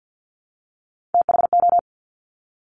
Morse code, heard as a keyed tone, about 50 wpm, 700 Hz. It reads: THO
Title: Top Three Letter Combinations in Words 50wpm